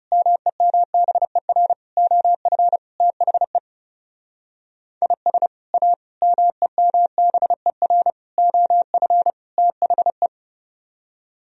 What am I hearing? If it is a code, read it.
Morse code, heard as a keyed tone, about 39 wpm, 700 Hz. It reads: MEMBER OF THE IS A MEMBER OF THE